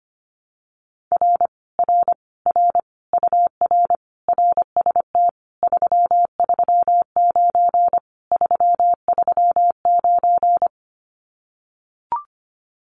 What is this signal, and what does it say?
Morse code, heard as a keyed tone, about 25 wpm, 700 Hz. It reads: R R R UR RST 339 339
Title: Common QSO Phrases Encore 25wpm